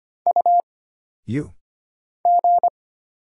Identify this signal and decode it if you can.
Morse code, heard as a keyed tone, about 25 wpm, 700 Hz. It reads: U G
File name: Single Character (Letter:Number:Punctuation:Prosign) - Mind-Melt - 25wpm